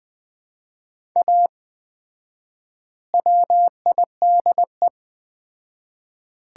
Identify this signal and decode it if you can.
Morse code, heard as a keyed tone, about 20 wpm, 700 Hz. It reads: A WIDE
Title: 4x Word-Spacing - Sets of 4 Words - 20wpm